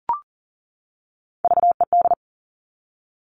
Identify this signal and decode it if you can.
Morse code, heard as a keyed tone, about 38 wpm, 700 Hz. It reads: VED
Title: Top Three Letter Combinations in Words 40wpm